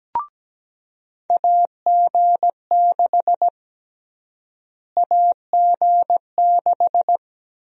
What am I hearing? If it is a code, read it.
Morse code, heard as a keyed tone, about 17 wpm, 700 Hz. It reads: AG6 AG6
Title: USA Call Sign Prefixes - Repeated 2x - 17wpm